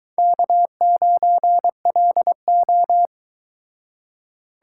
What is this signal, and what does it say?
Morse code, heard as a keyed tone, about 22 wpm, 700 Hz. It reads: K9LO